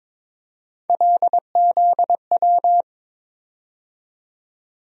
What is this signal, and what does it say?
Morse code, heard as a keyed tone, about 22 wpm, 700 Hz. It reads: LZW
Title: USA Call Sign Postfixes - 3 Letters 22wpm